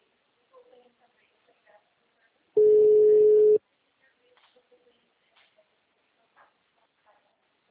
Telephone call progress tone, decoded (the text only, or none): ringback tone